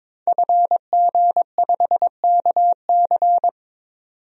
Morse code, heard as a keyed tone, about 21 wpm, 700 Hz. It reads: FG5KC
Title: DXCC-100 Call Signs - 5 Characters - 22wpm